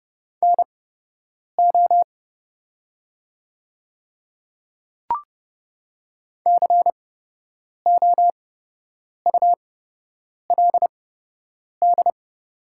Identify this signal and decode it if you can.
Morse code, heard as a keyed tone, about 30 wpm, spaced as for 8 wpm, 700 Hz. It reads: NO COULD
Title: 8x Character Spacing - Top 100 Words - 30wpm